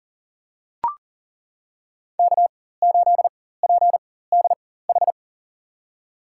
Morse code, heard as a keyed tone, about 40 wpm, spaced as for 19 wpm, 700 Hz. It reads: K8PDH
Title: USA Call Signs - 5 Characters - 4x Character Spacing - 40wpm